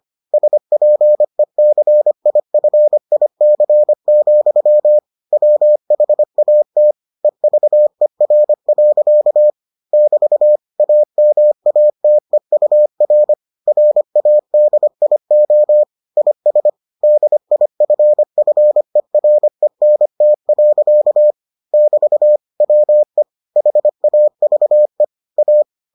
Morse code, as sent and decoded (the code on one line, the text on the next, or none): ... .--. -.-. .. ..-. .. -.-. --..-- / .-- .... .- - / . ...- . .-. .-.-.- / -...- / .- -- .- - . ..- .-. / .-. .- -.. .. --- / .. ... / -.. .. ..-. ..-. .-. -. - .-.-.- / -...- / .-- . / .... .- ...- . / .-
SPECIFIC, WHAT EVER. <BT> AMATEUR RADIO IS DIFFERENT. <BT> WE HAVE A